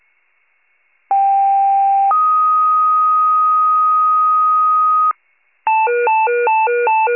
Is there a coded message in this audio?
Yes: two-tone page from 1.1 to 5.1 s, heard as a one-second tone, then a three-second tone, about 790 then 1250 Hz.